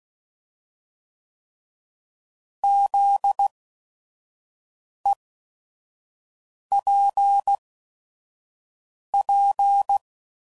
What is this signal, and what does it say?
Morse code, heard as a keyed tone, about 16 wpm, spaced as for 5 wpm, 800 Hz. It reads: ZEPP
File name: CW Code Groups @5WPM / fromSM's / Antennas